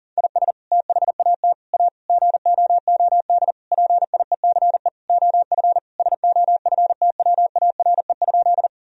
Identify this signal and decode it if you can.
Morse code, heard as a keyed tone, about 40 wpm, 700 Hz. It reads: IS THAT A GOOD PIECE OF SOFTWARE?
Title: Sentences from Top 1000 Words - Review Part 3 - 40wpm